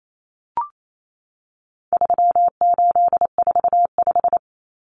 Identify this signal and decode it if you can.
Morse code, heard as a keyed tone, about 28 wpm, 700 Hz. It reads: 3845
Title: Common Cut Numbers - 4-Digit 28wpm